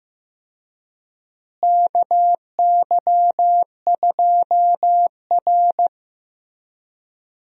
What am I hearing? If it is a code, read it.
Morse code, heard as a keyed tone, about 15 wpm, 700 Hz. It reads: KY2R
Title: USA Call Signs - 4 Characters - Repeated 3x - 15wpm